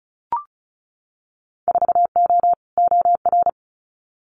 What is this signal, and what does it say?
Morse code, heard as a keyed tone, about 36 wpm, 700 Hz. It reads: 4O OR
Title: Winter Field Day Exchanges Encore 35wpm